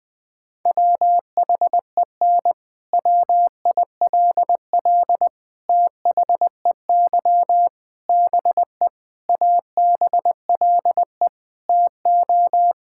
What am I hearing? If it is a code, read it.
Morse code, heard as a keyed tone, about 20 wpm, 700 Hz. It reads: WHEN WILL THEY BE ABLE TO
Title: Sentences from Top 1000 Words - Review Part 5 - 20wpm